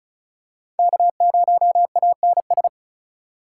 Morse code, heard as a keyed tone, about 36 wpm, 700 Hz. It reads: K0ANS
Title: USA Call Signs - 5 Characters - 4x Character Spacing - 35wpm